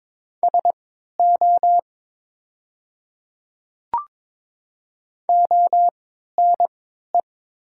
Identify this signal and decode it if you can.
Morse code, heard as a keyed tone, about 22 wpm, spaced as for 13 wpm, 700 Hz. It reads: SO ONE
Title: 3x Character Spacing - Top 100 Words - 22wpm